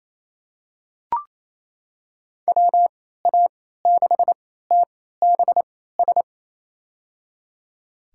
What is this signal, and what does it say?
Morse code, heard as a keyed tone, about 27 wpm, spaced as for 16 wpm, 700 Hz. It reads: WA6TBS
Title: USA Call Signs - 6 Characters - 3x Character Spacing - 28wpm